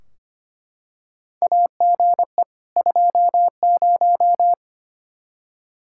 Morse code, heard as a keyed tone, about 25 wpm, 700 Hz. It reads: AGE 20
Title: Practice QSO - Exchange 3 - More Info - 25wpm